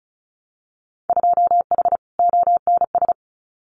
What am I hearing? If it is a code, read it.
Morse code, heard as a keyed tone, about 36 wpm, 700 Hz. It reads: 2H ONS
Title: Winter Field Day Exchanges Encore 35wpm